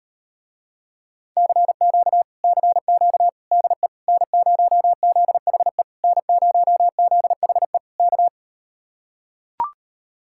Morse code, heard as a keyed tone, about 38 wpm, 700 Hz. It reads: CQ CQ DE N0ZHE N0ZHE K